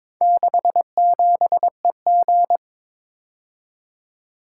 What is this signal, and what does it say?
Morse code, heard as a keyed tone, about 22 wpm, 700 Hz. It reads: 67EG
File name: practice / 22wpm / Four Letter-Number Combinations - 22wpm